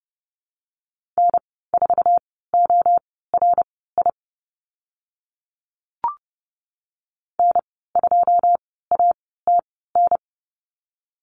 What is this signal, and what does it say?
Morse code, heard as a keyed tone, about 30 wpm, spaced as for 17 wpm, 700 Hz. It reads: N4ORI N2ATN
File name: Lesson 15g - Number 5 - Call Signs 30wpm